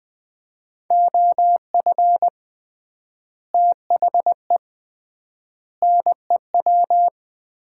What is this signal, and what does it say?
Morse code, heard as a keyed tone, about 20 wpm, 700 Hz. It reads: OF THE NEW
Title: 3x Word-Spacing - Sets of 5 Words - 20wpm